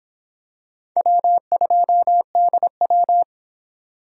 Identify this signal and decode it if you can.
Morse code, heard as a keyed tone, about 25 wpm, 700 Hz. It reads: W2DW